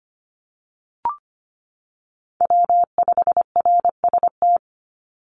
Morse code, heard as a keyed tone, about 25 wpm, 700 Hz. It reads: W5RST
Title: NA SOTA Chaser Callsigns 25wpm